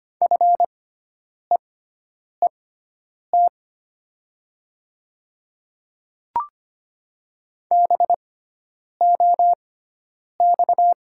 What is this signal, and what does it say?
Morse code, heard as a keyed tone, about 25 wpm, spaced as for 9 wpm, 700 Hz. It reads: FEET BOX
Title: 6x Character Spacing - Top 500 Words - 25wpm